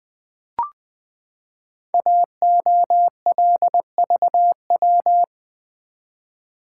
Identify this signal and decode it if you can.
Morse code, heard as a keyed tone, about 20 wpm, 700 Hz. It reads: AOLVW
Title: Code Groups - 5 Letters - 20wpm